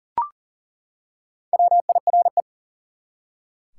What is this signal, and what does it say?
Morse code, heard as a keyed tone, about 39 wpm, 700 Hz. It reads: WIRE
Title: Top 180 QSO Elements 40wpm